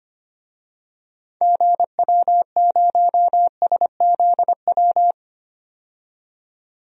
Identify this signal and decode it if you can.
Morse code, heard as a keyed tone, about 25 wpm, 700 Hz. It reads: GW0SZW
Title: DXCC-100 Call Signs - 6 Characters - 25wpm